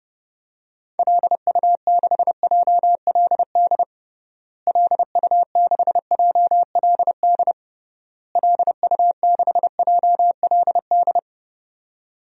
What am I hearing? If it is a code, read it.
Morse code, heard as a keyed tone, about 30 wpm, 700 Hz. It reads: LU6JLD LU6JLD LU6JLD
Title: DXCC-100 Call Signs - 6 Characters - Repeated 3x - 30wpm